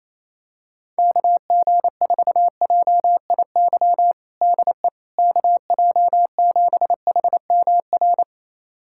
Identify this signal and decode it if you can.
Morse code, heard as a keyed tone, about 28 wpm, 700 Hz. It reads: KG4JIY DE KJ7HMR
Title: USA Callsign DE Callsign 28wpm